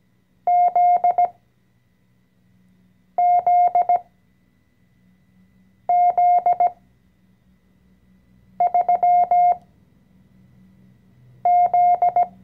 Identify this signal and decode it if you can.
Morse code, heard as a keyed tone, about 17 wpm, spaced as for 4 wpm, 710 Hz. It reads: ZZZ3Z
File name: CW-Online-Kurs / Anfänger / Lektion 014